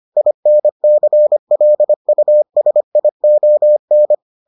Morse code, heard as a keyed tone, about 25 wpm, 600 Hz. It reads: INCLUSION